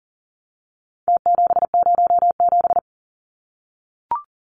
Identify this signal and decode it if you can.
Morse code, heard as a keyed tone, about 40 wpm, 700 Hz. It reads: T707
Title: Numbers - Contest Style - 4-Digit 40wpm